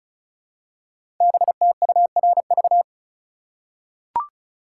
Morse code, heard as a keyed tone, about 36 wpm, 700 Hz. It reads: BTURV
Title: Code Groups - 5 Letters - 35wpm